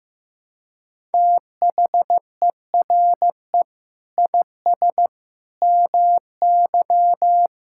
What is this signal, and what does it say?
Morse code, heard as a keyed tone, about 15 wpm, 700 Hz. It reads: THERE IS MY
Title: Sentences from Top 1000 Words - Review Part 6 - 15wpm